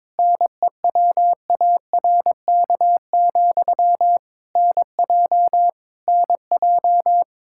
Morse code, heard as a keyed tone, about 22 wpm, 700 Hz. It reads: NEWARK, NJ NJ
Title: K1USN SST QSO Practice 22wpm